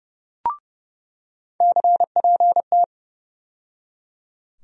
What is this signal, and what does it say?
Morse code, heard as a keyed tone, about 30 wpm, 700 Hz. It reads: CPT
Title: USA Call Sign Postfixes - 3 Letters 30wpm